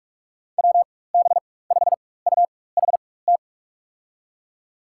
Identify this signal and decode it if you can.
Morse code, heard as a keyed tone, about 46 wpm, spaced as for 21 wpm, 700 Hz. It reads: WB5UHT
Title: USA Call Signs - 6 Characters - 4x Character Spacing - 45wpm